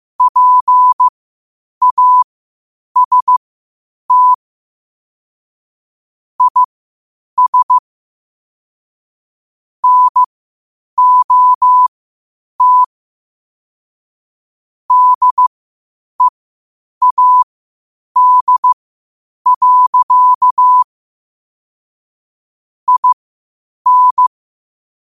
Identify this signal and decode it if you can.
Morse code, heard as a keyed tone, about 15 wpm, spaced as for 8 wpm, 1000 Hz. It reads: PAST IS NOT DEAD. IN
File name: qotdfiles / 8wpm